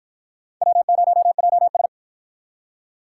Morse code, heard as a keyed tone, about 47 wpm, 700 Hz. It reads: W0JS